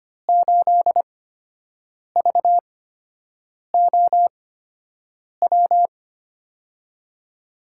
Morse code, heard as a keyed tone, about 25 wpm, spaced as for 7 wpm, 700 Hz. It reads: 8VOW
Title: USA Call Signs - 6 Characters - 8x Character Spacing - 25wpm